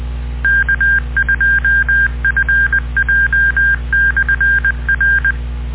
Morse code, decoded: K2FJ/R